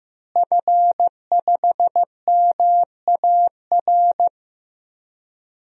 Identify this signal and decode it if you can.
Morse code, heard as a keyed tone, about 15 wpm, 700 Hz. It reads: F5MAR
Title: DXCC-100 Call Signs - 5 Characters - Repeated 2x - 15wpm